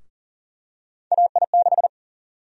Morse code, heard as a keyed tone, about 38 wpm, 700 Hz. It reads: AI6
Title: USA Call Sign Prefixes - Repeated 2x - 40wpm